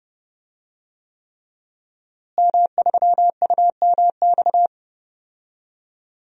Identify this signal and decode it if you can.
Morse code, heard as a keyed tone, about 30 wpm, 700 Hz. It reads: M3UMX
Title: DXCC-100 Call Signs - 5 Characters - Repeated 3x - 30wpm